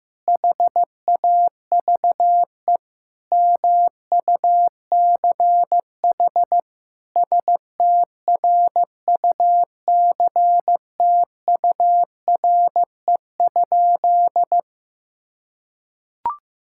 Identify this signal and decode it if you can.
Morse code, heard as a keyed tone, about 15 wpm, 700 Hz. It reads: HAVE MUCH STRUCTURE?
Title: Sentences from Top 1000 Words - Review Part 6 - 15wpm